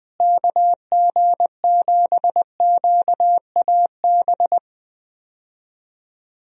Morse code, heard as a keyed tone, about 20 wpm, 700 Hz. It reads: KG7QAB
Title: USA Call Signs - 6 Characters - Repeated 3x - 20wpm